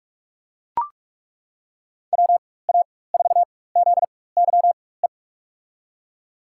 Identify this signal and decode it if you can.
Morse code, heard as a keyed tone, about 42 wpm, spaced as for 20 wpm, 700 Hz. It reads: WA4ZYE
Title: USA Call Signs - 6 Characters - 4x Character Spacing - 45wpm